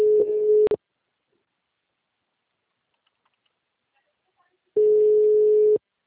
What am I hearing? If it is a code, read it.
Ringback tone (Europe, 425 Hz).